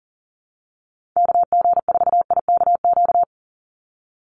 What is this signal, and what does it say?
Morse code, heard as a keyed tone, about 38 wpm, 700 Hz. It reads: KG4IKQ
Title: Call Signs Repeated 2x - 40wpm